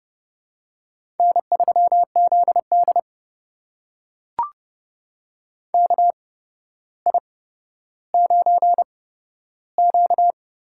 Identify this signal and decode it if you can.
Morse code, heard as a keyed tone, about 30 wpm, spaced as for 13 wpm, 700 Hz. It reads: N3ZD KI9Q
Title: USA Call Signs - 4 Characters - 8x Character Spacing - 30wpm